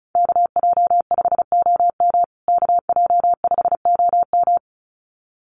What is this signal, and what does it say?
Morse code, heard as a keyed tone, about 34 wpm, 700 Hz. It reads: KJ5OM KJ5OM
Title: USA Call Signs Repeated 2x 35wpm